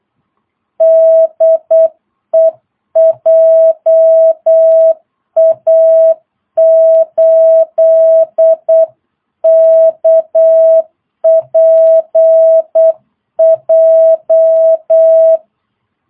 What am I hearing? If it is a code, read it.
Morse code, heard as a keyed tone, about 8 wpm, 660 Hz. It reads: DE JA8KPJ
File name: jh8cbh / 2022cw